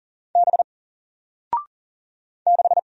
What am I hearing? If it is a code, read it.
Morse code, heard as a keyed tone, about 39 wpm, 700 Hz. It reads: B 6
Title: Single Letter-Number Alt 40wpm